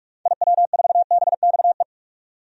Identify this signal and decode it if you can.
Morse code, heard as a keyed tone, about 46 wpm, 700 Hz. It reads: IW4BXE
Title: DXCC-100 Call Signs - 6 Characters - 45wpm